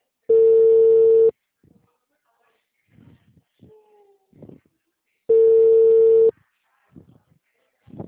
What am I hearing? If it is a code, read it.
Ringback tone (China, 450 Hz), with speech.
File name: out- -2001